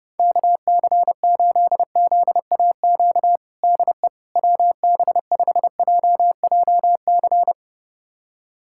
Morse code, heard as a keyed tone, about 32 wpm, 700 Hz. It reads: KC8ZAQ DE WB5JJC
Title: USA Callsign DE Callsign 30wpm